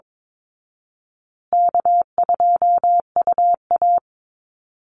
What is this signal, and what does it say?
Morse code, heard as a keyed tone, about 22 wpm, 700 Hz. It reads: K2UA
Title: Call Signs Repeated 2x - 22wpm